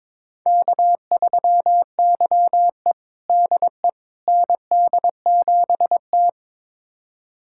Morse code, heard as a keyed tone, about 21 wpm, 700 Hz. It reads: K3YE DE ND7T
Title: USA Callsign DE Callsign - 4 Characters - 22wpm